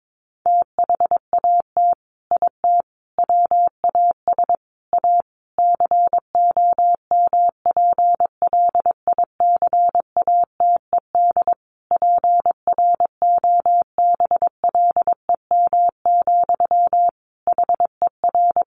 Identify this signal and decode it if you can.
Morse code, heard as a keyed tone, about 22 wpm, 700 Hz. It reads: THAT IT WAS A COMPLICATED PROBLEM, HER
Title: Sentences from Top 1200 Words 22wpm